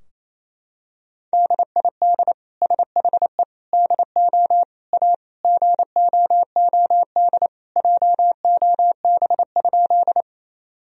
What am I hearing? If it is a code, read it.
Morse code, heard as a keyed tone, about 28 wpm, 700 Hz. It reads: DID SHE DO A GOOD JOB?
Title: Sentences from Top 1000 Words - Review Part 3 - 28wpm